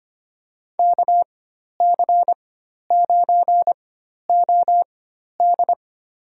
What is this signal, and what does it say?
Morse code, heard as a keyed tone, about 25 wpm, spaced as for 12 wpm, 700 Hz. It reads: KC9OD
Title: USA Call Signs - 4x Character Spacing - 25wpm